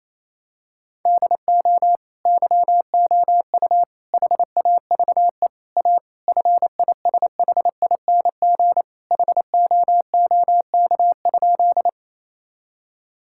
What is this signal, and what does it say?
Morse code, heard as a keyed tone, about 27 wpm, 700 Hz. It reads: DO YOU HAVE A FISHING HOOK?
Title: Sentences from Top 1600 Words 28wpm